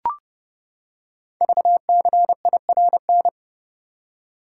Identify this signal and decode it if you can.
Morse code, heard as a keyed tone, about 30 wpm, 700 Hz. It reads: VCIRN
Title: Code Groups - 5 Letters - 30wpm